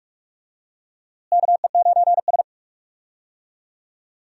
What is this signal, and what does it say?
Morse code, heard as a keyed tone, about 46 wpm, 700 Hz. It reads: KE9S
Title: USA Call Signs - 4 Characters - Repeated 2x - 45wpm